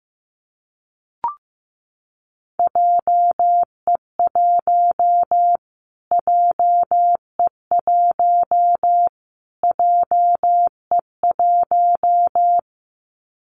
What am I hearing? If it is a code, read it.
Morse code, heard as a keyed tone, about 15 wpm, 700 Hz. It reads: JE1 JE1 JE1
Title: Callsign Prefixes for Top 25 DXCC - Repeated 3x - 15wpm